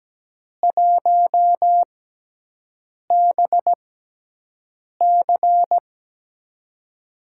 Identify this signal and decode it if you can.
Morse code, heard as a keyed tone, about 17 wpm, 700 Hz. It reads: 1 B C
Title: USA Call Signs - 6x Character Spacing - 17wpm